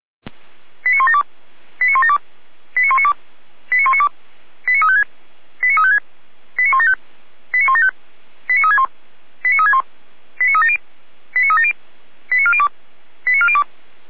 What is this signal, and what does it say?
Five-tone selective call (ZVEI), heard as quick runs of five tones, tones of about 80 ms. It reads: 89172 89172 89192 89192 89356 89356 89175 89175 89361 89361 89370 89370 89402 89402